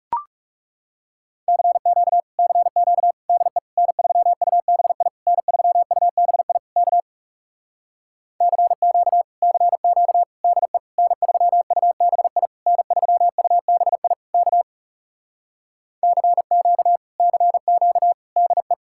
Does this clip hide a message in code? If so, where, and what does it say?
Yes: Morse code from 1.5 to 18.7 s, heard as a keyed tone, about 40 wpm, 700 Hz. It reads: CQ CQ DE N3UBI N3UBI K CQ CQ DE N3UBI N3UBI K CQ CQ DE